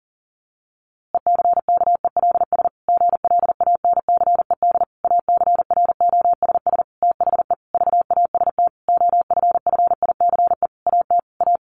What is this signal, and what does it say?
Morse code, heard as a keyed tone, about 40 wpm, 700 Hz. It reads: ECKELS GLANCED ACROSS THE VAST OFFICE AT A
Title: A Sound Of Thunder 40wpm